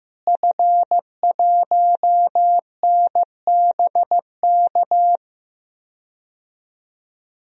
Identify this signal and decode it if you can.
Morse code, heard as a keyed tone, about 15 wpm, 700 Hz. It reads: F1NBK
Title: DXCC-100 Call Signs - 5 Characters - Repeated 2x - 15wpm